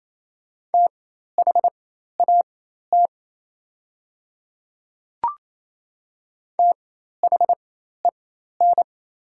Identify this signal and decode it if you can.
Morse code, heard as a keyed tone, about 27 wpm, spaced as for 13 wpm, 690 Hz. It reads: THAT THEN
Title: 4x Character Spacing - Top 200 Words - 28wpm